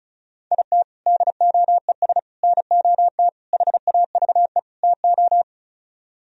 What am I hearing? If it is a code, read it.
Morse code, heard as a keyed tone, about 43 wpm, 700 Hz. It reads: IT DOES NOT HAVE TO
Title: Speed-Racing - Sets of 5 Words 35wpm